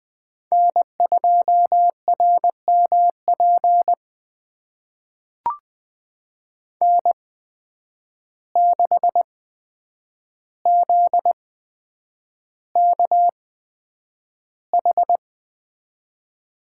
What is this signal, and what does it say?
Morse code, heard as a keyed tone, about 20 wpm, spaced as for 9 wpm, 700 Hz. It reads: N2RMP N6ZKH
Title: USA Call Signs - 5 Characters - 8x Character Spacing - 20wpm